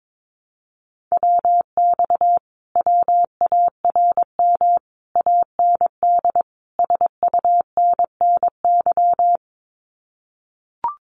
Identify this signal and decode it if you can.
Morse code, heard as a keyed tone, about 22 wpm, 700 Hz. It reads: WX WARM AND SUNNY
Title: Common QSO Phrases Encore 22wpm